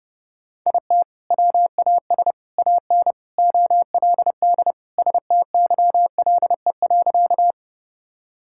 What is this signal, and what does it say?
Morse code, heard as a keyed tone, about 32 wpm, 700 Hz. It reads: IT WAS AN OLD STYLE.
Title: Sentences from Top 1000 Words - Review Part 5 - 30wpm